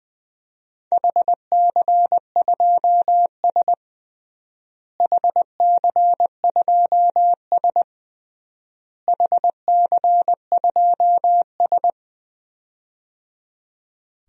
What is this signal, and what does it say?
Morse code, heard as a keyed tone, about 21 wpm, 700 Hz. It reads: HC2S HC2S HC2S
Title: DXCC-100 Call Signs - 4 Characters - Repeated 3x - 20wpm